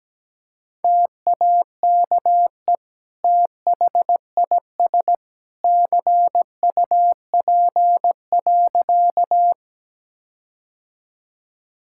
Morse code, heard as a keyed tone, about 17 wpm, 700 Hz. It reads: TAKE THIS CUP.
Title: Sentences from Top 1600 Words 17wpm